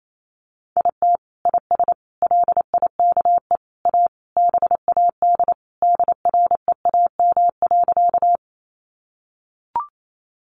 Morse code, heard as a keyed tone, about 28 wpm, 700 Hz. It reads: IT IS LIKE A BAD DREAM.
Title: Sentences from Top 1500 Words 28wpm